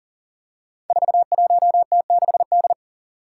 Morse code, heard as a keyed tone, about 40 wpm, 700 Hz. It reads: 41T6D
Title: Five Challenging Letter-Number Combinations 40wpm